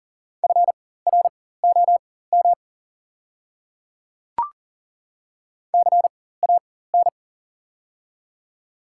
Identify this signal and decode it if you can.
Morse code, heard as a keyed tone, about 40 wpm, spaced as for 19 wpm, 700 Hz. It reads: FROM CAN
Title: 4x Character Spacing - Top 100 Words - 40wpm